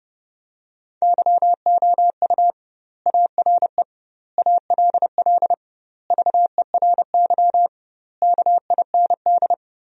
Morse code, heard as a keyed tone, about 30 wpm, 700 Hz. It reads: YOU ARE ALL VERY KIND
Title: 2x Word-Spacing - Sentences from Top 200 Words - 30wpm